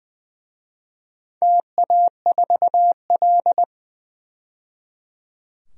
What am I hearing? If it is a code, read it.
Morse code, heard as a keyed tone, about 20 wpm, 700 Hz. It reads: TA4L
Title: DXCC-100 Call Signs - 4 Characters - 20wpm